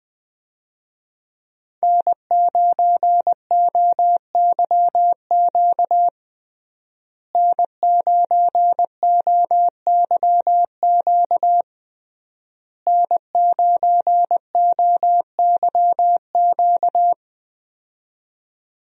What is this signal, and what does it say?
Morse code, heard as a keyed tone, about 20 wpm, 700 Hz. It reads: N9OYQ N9OYQ N9OYQ
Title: USA Call Signs - 5 Characters - Repeated 3x - 20wpm